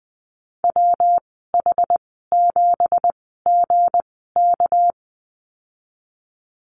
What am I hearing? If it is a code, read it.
Morse code, heard as a keyed tone, about 20 wpm, spaced as for 14 wpm, 700 Hz. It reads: WH7GK
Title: 213 Lesson 34g - Number 8 - Call Signs 20wpm